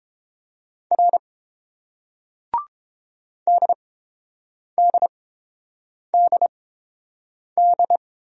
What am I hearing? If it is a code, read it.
Morse code, heard as a keyed tone, about 30 wpm, spaced as for 7 wpm, 700 Hz. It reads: R DDDD